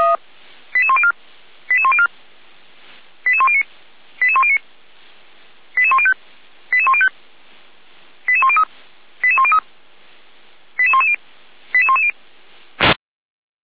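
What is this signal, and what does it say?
Five-tone selective call (ZVEI), heard as quick runs of five tones, tones of about 70 ms. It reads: 88104 88104 88109 88109 88105 88105 88103 88103 88110 88110
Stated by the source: ZVEI five-tone selective call received by SDR, fire brigade Feuerwehr Orlamünde.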